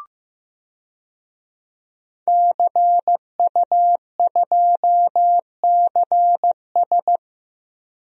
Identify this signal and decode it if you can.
Morse code, heard as a keyed tone, about 15 wpm, 700 Hz. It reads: CU2CS
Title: DXCC-100 Call Signs - 5 Characters - Repeated 3x - 15wpm